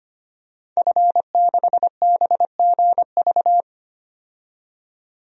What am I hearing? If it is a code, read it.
Morse code, heard as a keyed tone, about 25 wpm, 700 Hz. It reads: F6BGV
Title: DXCC-100 Call Signs - 5 Characters - 25wpm